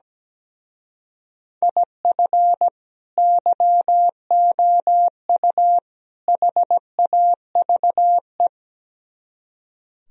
Morse code, heard as a keyed tone, about 17 wpm, 700 Hz. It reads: IF YOU HAVE
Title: Speed-Racing - Sets of 3 Words 17wpm